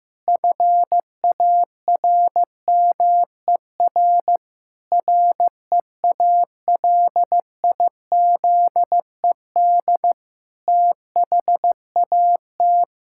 Morse code, heard as a keyed tone, about 15 wpm, 700 Hz. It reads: FARMER REALIZED THAT
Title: Pangram Challenge v2 - 15wpm